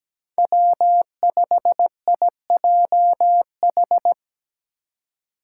Morse code, heard as a keyed tone, about 17 wpm, 700 Hz. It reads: W5IJH
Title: K1USN SST QSO Practice - 17wpm